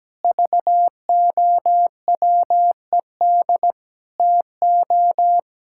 Morse code, heard as a keyed tone, about 17 wpm, 700 Hz. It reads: VOWED TO
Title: Pangram Challenge v2 - 17wpm